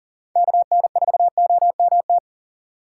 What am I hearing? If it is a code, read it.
Morse code, heard as a keyed tone, about 40 wpm, 700 Hz. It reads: KN4OMT
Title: USA Call Signs - 6 Characters - 4x Character Spacing - 40wpm